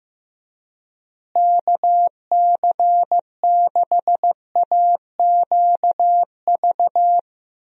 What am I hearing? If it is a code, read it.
Morse code, heard as a keyed tone, about 15 wpm, 700 Hz. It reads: KC6AQV
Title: USA Call Signs - 6 Characters - Repeated 2x - 15wpm